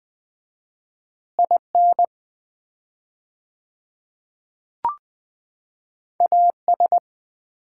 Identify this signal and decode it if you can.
Morse code, heard as a keyed tone, about 20 wpm, 690 Hz. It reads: IN AS
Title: 4x Word-Spacing - Sets of 2 Words - 20wpm